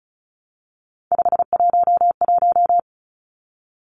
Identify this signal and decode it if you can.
Morse code, heard as a keyed tone, about 34 wpm, 700 Hz. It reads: 511
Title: Numbers - 3-Digit 35wpm